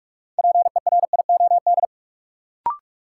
Morse code, heard as a keyed tone, about 46 wpm, 700 Hz. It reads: PERIOD
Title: Sets of 5 Words - New Word Drill - 45wpm